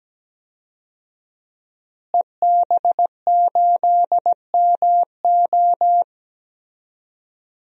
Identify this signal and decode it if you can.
Morse code, heard as a keyed tone, about 17 wpm, 700 Hz. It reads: EB8MO
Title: DXCC-100 Call Signs - 5 Characters - Repeated 2x - 17wpm